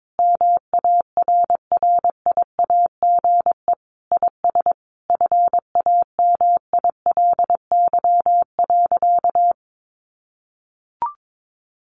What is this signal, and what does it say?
Morse code, heard as a keyed tone, about 22 wpm, 700 Hz. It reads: MARRIAGE IS FAMILY.
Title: Sentences from Top 800 Words 22wpm